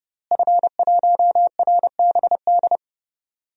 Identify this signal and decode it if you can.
Morse code, heard as a keyed tone, about 30 wpm, 700 Hz. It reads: F1RBD
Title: DXCC-100 Call Signs - 5 Characters - Repeated 2x - 30wpm